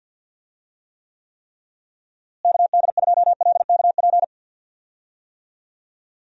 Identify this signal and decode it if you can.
Morse code, heard as a keyed tone, about 52 wpm, 700 Hz. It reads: KD2LKP
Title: USA Call Signs - 6 Characters - Repeated 3x - 50wpm